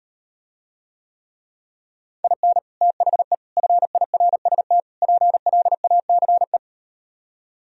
Morse code, heard as a keyed tone, about 38 wpm, 700 Hz. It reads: IN THE FIRST PLACE